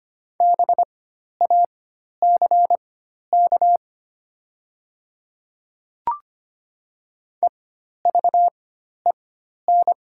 Morse code, heard as a keyed tone, about 25 wpm, spaced as for 12 wpm, 700 Hz. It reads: BACK EVEN
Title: 4x Character Spacing - Top 300 Words - 25wpm